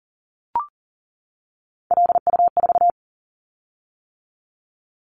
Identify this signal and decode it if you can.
Morse code, heard as a keyed tone, about 40 wpm, 700 Hz. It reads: LU4
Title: Callsign Prefixes for Top 100 DXCC - 40wpm